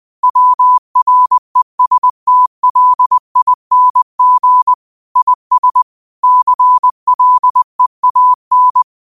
Morse code, heard as a keyed tone, about 20 wpm, 1000 Hz. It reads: WRESTLING IS CLEAN
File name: qotdfiles / 20wpm